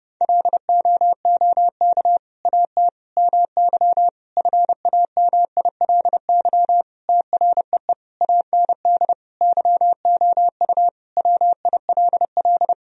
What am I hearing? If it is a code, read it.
Morse code, heard as a keyed tone, about 32 wpm, 700 Hz. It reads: LOOK AT MY FAMILY TREE AND YOU WILL
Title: Sentences from Top 1000 Words - Review Part 1 - 30wpm